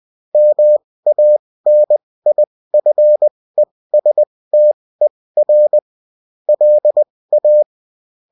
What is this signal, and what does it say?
Morse code, heard as a keyed tone, about 20 wpm, spaced as for 16 wpm, 600 Hz. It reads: MANIFESTER LA